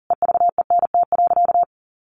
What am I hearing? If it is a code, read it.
Morse code, heard as a keyed tone, about 40 wpm, 700 Hz. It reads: EVENT.
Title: Sentences from Top 1400 Words 40wpm